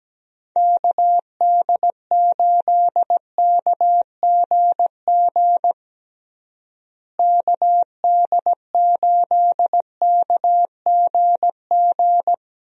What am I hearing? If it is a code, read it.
Morse code, heard as a keyed tone, about 17 wpm, 700 Hz. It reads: KD8KGG KD8KGG
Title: USA Call Signs - 6 Characters - Repeated 3x - 17wpm